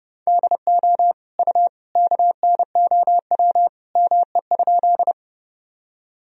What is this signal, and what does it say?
Morse code, heard as a keyed tone, about 30 wpm, 700 Hz. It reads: DO U KNOW ME?
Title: Sentences from Top 200 Words and Abbreviations 30wpm